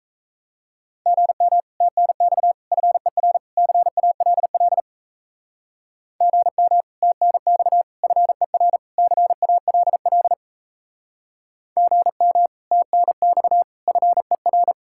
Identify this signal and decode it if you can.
Morse code, heard as a keyed tone, about 34 wpm, 700 Hz. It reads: GM TNX FER CALL GM TNX FER CALL GM TNX FER
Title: Speed-Racing Common QSO Phrases - 28wpm